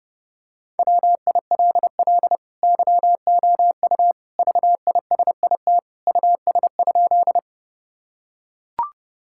Morse code, heard as a keyed tone, about 30 wpm, 700 Hz. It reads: WILL YOU VISIT US?
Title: Sentences from Top 1000 Words - Review Part 6 - 30wpm